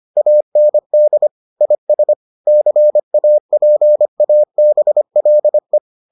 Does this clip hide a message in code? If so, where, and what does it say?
Yes: Morse code from 0.2 to 5.8 s, heard as a keyed tone, about 25 wpm, 600 Hz. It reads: AND IS CAPABLE